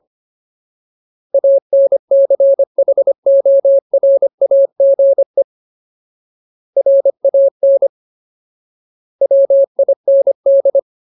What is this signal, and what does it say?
Morse code, heard as a keyed tone, about 25 wpm, 550 Hz. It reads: ANCHORAGE RAN WIND
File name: Ultimate Word List / 25WPM Ultimate